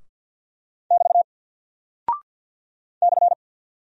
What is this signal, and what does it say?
Morse code, heard as a keyed tone, about 48 wpm, 700 Hz. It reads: <BT> /
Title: Single Character (Letter:Number:Punctuation:Prosign) Alt 50wpm